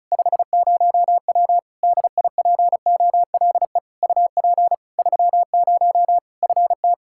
Morse code, heard as a keyed tone, about 36 wpm, 700 Hz. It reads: 50W DIPOLE UP 30 FT
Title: Practice QSO - Exchange 2 - Info - 35wpm